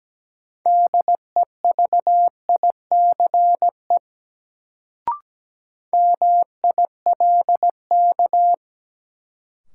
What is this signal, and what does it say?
Morse code, heard as a keyed tone, about 17 wpm, 700 Hz. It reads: DEVICE MILK
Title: Sentences from Top 1500 Words - New Word Drill - 17wpm